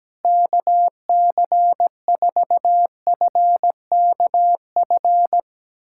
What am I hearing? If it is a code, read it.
Morse code, heard as a keyed tone, about 17 wpm, 700 Hz. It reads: KC4FKF
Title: USA Callsign DE Callsign - 6 Characters - 17wpm